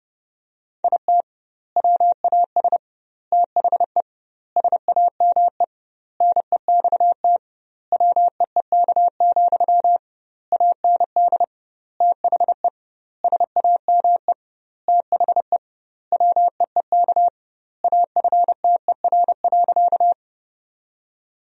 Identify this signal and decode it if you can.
Morse code, heard as a keyed tone, about 28 wpm, 700 Hz. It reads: IT WAS THE SAME NEXT WEEK, AND THE SAME THE WEEK AFTER.
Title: 2x Word-Spacing - Sentences from Top 500 Words - 30wpm